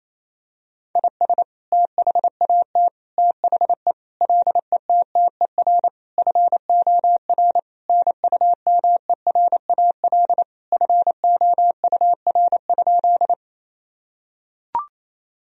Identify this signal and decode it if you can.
Morse code, heard as a keyed tone, about 28 wpm, 700 Hz. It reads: IS THAT THE LETTER FOR NUMERAL FOUR?
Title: Sentences from Top 1000 Words - Review Part 2 - 28wpm